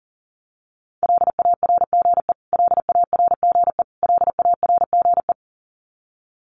Morse code, heard as a keyed tone, about 40 wpm, 700 Hz. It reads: LARGE LARGE LARGE
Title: Top 500 Words Repeated 40wpm